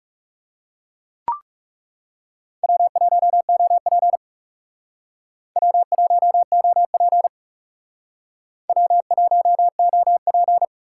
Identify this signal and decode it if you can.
Morse code, heard as a keyed tone, about 36 wpm, 700 Hz. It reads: W1OP W1OP W1OP
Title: Speed Racing - USA 1x2 Call Signs 30wpm